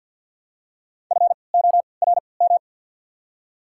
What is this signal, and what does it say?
Morse code, heard as a keyed tone, about 54 wpm, spaced as for 29 wpm, 700 Hz. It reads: FORM
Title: 3x Character Spacing - Top 500 Words - 50wpm